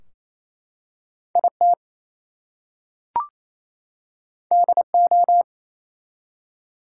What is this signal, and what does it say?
Morse code, heard as a keyed tone, about 27 wpm, 700 Hz. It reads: IT DO
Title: Two Letter Words - 28wpm